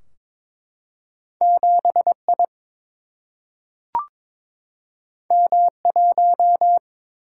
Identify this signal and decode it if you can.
Morse code, heard as a keyed tone, about 22 wpm, 700 Hz. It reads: 7 I M 1
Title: Single-Letter Single-Number 22wpm